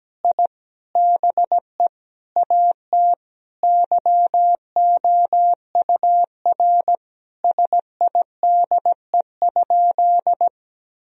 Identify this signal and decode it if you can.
Morse code, heard as a keyed tone, about 17 wpm, 700 Hz. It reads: I BE AT YOUR SIDE?
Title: Sentences from Top 1000 Words - Review Part 1 - 17wpm